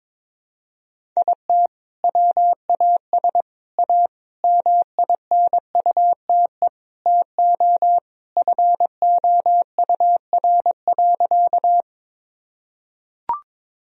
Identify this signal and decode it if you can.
Morse code, heard as a keyed tone, about 22 wpm, 700 Hz. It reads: IT WAS A MINUTE TO FOUR.
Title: Sentences from Top 1000 Words - Review Part 1 - 22wpm